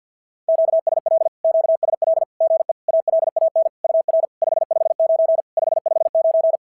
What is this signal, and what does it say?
Morse code, heard as a keyed tone, about 48 wpm, 650 Hz. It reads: QSL QSL GE ALAN UR 559 559
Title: SOTA QSO Practice 50wpm